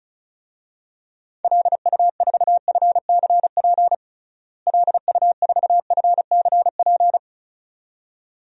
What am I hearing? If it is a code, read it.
Morse code, heard as a keyed tone, about 34 wpm, 700 Hz. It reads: LU4FCP LU4FCP
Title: DXCC-100 Call Signs - 6 Characters - Repeated 2x - 35wpm